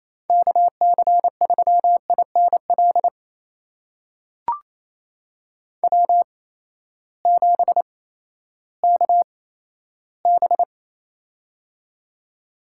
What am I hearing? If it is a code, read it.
Morse code, heard as a keyed tone, about 27 wpm, spaced as for 14 wpm, 700 Hz. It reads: KC3INL W7KB
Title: USA Call Signs - 8x Character Spacing - 28wpm